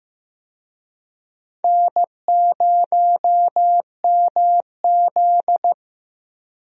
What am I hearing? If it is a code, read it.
Morse code, heard as a keyed tone, about 15 wpm, 700 Hz. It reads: N0MZ
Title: USA Call Signs - 4 Characters - Repeated 2x - 15wpm